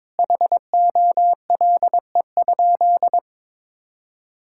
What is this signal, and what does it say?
Morse code, heard as a keyed tone, about 22 wpm, 700 Hz. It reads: HOLE?
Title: Sentences from Top 1600 Words 22wpm